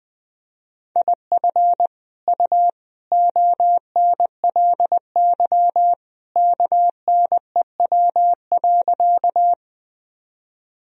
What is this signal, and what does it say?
Morse code, heard as a keyed tone, about 20 wpm, 700 Hz. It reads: IF U ONLY KNEW.
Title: Sentences from Top 400 Words and Abbreviations 20wpm